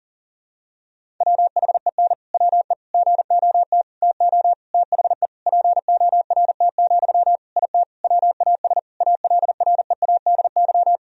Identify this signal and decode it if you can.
Morse code, heard as a keyed tone, about 37 wpm, 700 Hz. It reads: WHEN WE GOT TO THE PORT, IT WAS ALREADY
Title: Sentences from Top 1000 Words - Review Part 1 - 40wpm